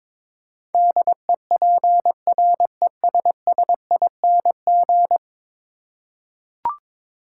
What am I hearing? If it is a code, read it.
Morse code, heard as a keyed tone, about 22 wpm, 700 Hz. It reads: DEPRESSING
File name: practice / 22wpm / 10 Letter Words - First 3 Letters Given - 22wpm